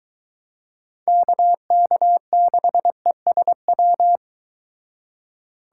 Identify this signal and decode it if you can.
Morse code, heard as a keyed tone, about 22 wpm, 700 Hz. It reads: KK6ESW